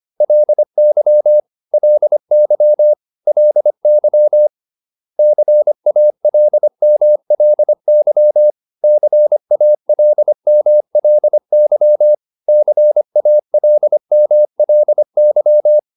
Morse code, heard as a keyed tone, about 25 wpm, 600 Hz. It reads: LY LY LY CALMLY CALMLY CALMLY